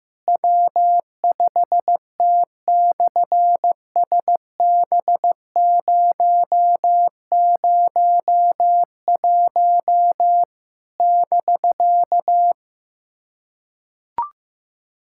Morse code, heard as a keyed tone, about 15 wpm, 700 Hz. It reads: W5T/SB001 <BK>
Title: SOTA QSO Practice 15wpm